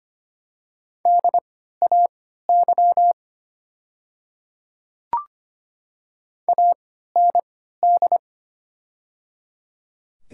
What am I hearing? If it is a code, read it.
Morse code, heard as a keyed tone, about 25 wpm, spaced as for 14 wpm, 700 Hz. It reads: DAY AND
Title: 3x Character Spacing - Top 100 Words - 25wpm